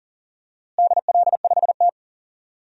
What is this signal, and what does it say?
Morse code, heard as a keyed tone, about 40 wpm, 700 Hz. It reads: DL5T
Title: DXCC-100 Call Signs - 4 Characters - Repeated 3x - 40wpm